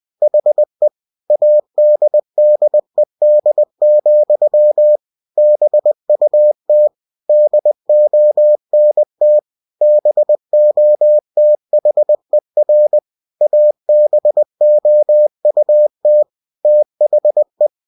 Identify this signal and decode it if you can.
Morse code, heard as a keyed tone, about 20 wpm, 600 Hz. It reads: HE ADDED, BUT DONT BOTHER ABOUT THE
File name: cw-books / winnie-the-pooh / winnie-20wpm-page-0015